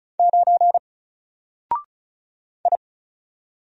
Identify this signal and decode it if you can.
Morse code, heard as a keyed tone, about 34 wpm, 700 Hz. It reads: 9 I
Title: Single Character (Letter:Number:Punctuation:Prosign) Alt 35wpm